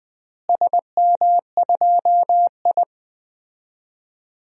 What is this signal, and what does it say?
Morse code, heard as a keyed tone, about 20 wpm, 700 Hz. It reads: SM2I
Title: DXCC-100 Call Signs - 4 Characters - Repeated 3x - 20wpm